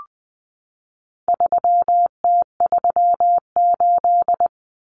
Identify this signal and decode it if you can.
Morse code, heard as a keyed tone, about 20 wpm, 700 Hz. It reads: 3T38
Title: Common Cut Numbers - 4-Digit 20wpm